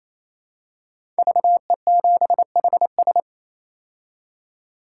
Morse code, heard as a keyed tone, about 28 wpm, 700 Hz. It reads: VE7HS
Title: K1USN SST QSO Practice - 28wpm